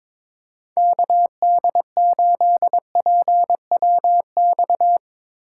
Morse code, heard as a keyed tone, about 23 wpm, 700 Hz. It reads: KD8PWX
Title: USA Call Signs - 8x Character Spacing - 22wpm